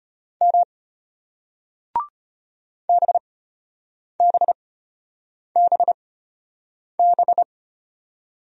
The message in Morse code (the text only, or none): M BBBB